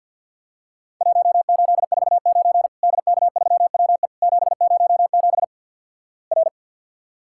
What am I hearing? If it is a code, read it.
Morse code, heard as a keyed tone, about 52 wpm, 700 Hz. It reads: 1849 DK3WE 707 R
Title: Worked All Europe DX Contest - QTC Exchange with Repeats - 50wpm